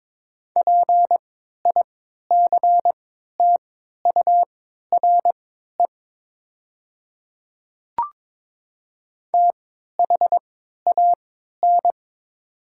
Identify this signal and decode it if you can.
Morse code, heard as a keyed tone, about 22 wpm, spaced as for 12 wpm, 700 Hz. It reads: PICTURE THAN
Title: 3x Character Spacing - Top 300 Words - 22wpm